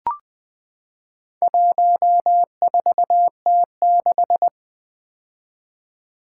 Morse code, heard as a keyed tone, about 20 wpm, 700 Hz. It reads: 14T6
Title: Four Letter-Number Combinations - 20wpm